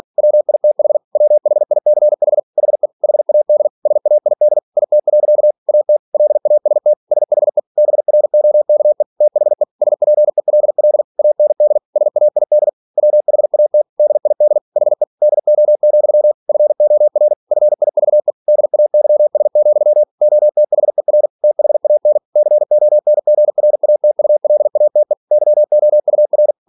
Morse code, heard as a keyed tone, about 46 wpm, 600 Hz. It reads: WITH WHICH HE HAD SAID IT. AT LAST SHE BROKE THE SPELL AND SAID WHAT DID HE DO, FOR FIVE DAYS, OTHER THAN CONGRATULATE YOUR
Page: Never 47wpm